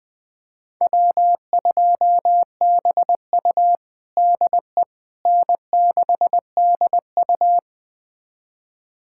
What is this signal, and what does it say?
Morse code, heard as a keyed tone, about 20 wpm, 700 Hz. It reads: W2BU DE N6DU
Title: USA Callsign DE Callsign - 4 Characters - 20wpm